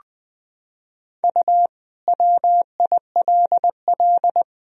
Morse code, heard as a keyed tone, about 20 wpm, 700 Hz. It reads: U WILL
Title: Sentences from Top 300 Words and Abbreviations 20wpm